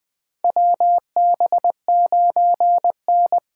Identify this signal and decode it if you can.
Morse code, heard as a keyed tone, about 20 wpm, 700 Hz. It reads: WB9N
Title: USA Call Signs - 6 Characters - 8x Character Spacing - 20wpm